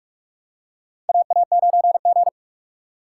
Morse code, heard as a keyed tone, about 42 wpm, 700 Hz. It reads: AA9G
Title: USA Call Signs - 6x Character Spacing - 45wpm